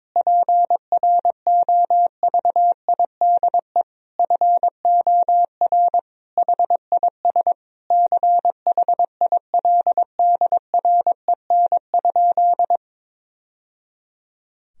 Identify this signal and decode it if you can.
Morse code, heard as a keyed tone, about 22 wpm, 700 Hz. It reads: PROVIDE FOR HIS CHILDREN?
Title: Sentences from Top 1000 Words - Review Part 3 - 22wpm